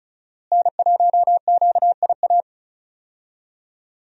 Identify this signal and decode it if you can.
Morse code, heard as a keyed tone, about 34 wpm, 700 Hz. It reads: N1QIA
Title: USA Call Signs - 5 Characters - Repeated 3x - 35wpm